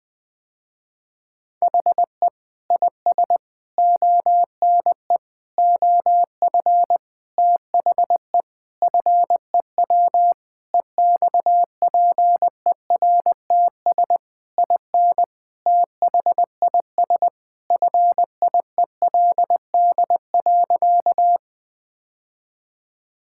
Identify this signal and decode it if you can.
Morse code, heard as a keyed tone, about 20 wpm, 700 Hz. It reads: HE IS ONE OF THE FEW EXPERTS IN THIS FIELD.
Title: Sentences with 4-Word Phrases - Spoken Beginning - 20wpm